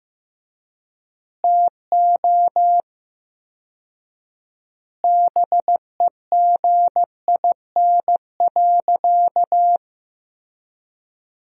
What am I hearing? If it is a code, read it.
Morse code, heard as a keyed tone, about 16 wpm, 700 Hz. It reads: TO BEGIN.
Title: 4x Word-Spacing - Sentences from Top 300 Words - 15wpm